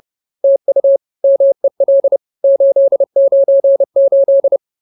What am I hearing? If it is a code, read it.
Morse code, heard as a keyed tone, about 30 wpm, 550 Hz. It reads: TU MEL 898